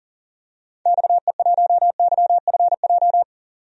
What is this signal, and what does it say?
Morse code, heard as a keyed tone, about 40 wpm, 700 Hz. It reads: XE1YFJ
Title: DXCC-100 Call Signs - 6 Characters - 40wpm